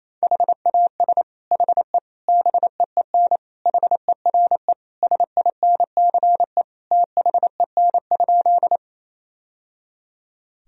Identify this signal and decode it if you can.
Morse code, heard as a keyed tone, about 27 wpm, 700 Hz. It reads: HAS HE BEEN HERE SINCE THEN?
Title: Sentences from Top 1000 Words - Review Part 1 - 28wpm